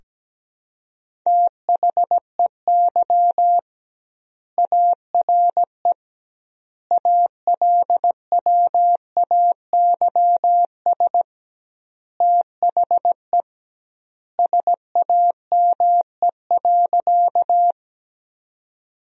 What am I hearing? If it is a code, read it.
Morse code, heard as a keyed tone, about 17 wpm, 700 Hz. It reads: THEY ARE ALWAYS THE SAME.
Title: 2x Word-Spacing - Sentences from Top 300 Words - 17wpm